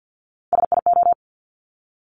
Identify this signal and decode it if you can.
Morse code, heard as a keyed tone, about 50 wpm, 700 Hz. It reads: SIO
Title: Top Three Letter Combinations in Words 50wpm